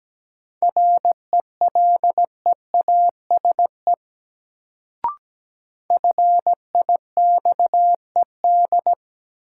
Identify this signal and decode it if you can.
Morse code, heard as a keyed tone, about 17 wpm, 700 Hz. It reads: RELEASE FIXED
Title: Sentences from Top 1100 Words - New Word Drill Alt - 17wpm